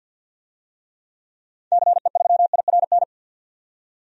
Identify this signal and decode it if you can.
Morse code, heard as a keyed tone, about 46 wpm, 700 Hz. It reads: CE3IRN
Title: DXCC-100 Call Signs - 6 Characters - 50wpm